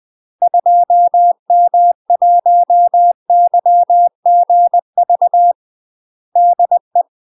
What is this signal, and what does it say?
Morse code, heard as a keyed tone, about 20 wpm, 700 Hz. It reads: 2M1YGV DE
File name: UK MASTER QSOs TRAINING / 20WPM / UK MASTER QSOs 0011